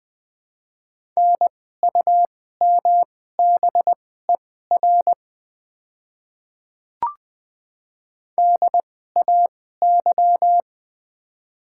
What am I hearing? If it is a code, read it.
Morse code, heard as a keyed tone, about 21 wpm, spaced as for 15 wpm, 700 Hz. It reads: NUMBER DAY
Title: 2x Character Spacing - Top 100 Words - 20wpm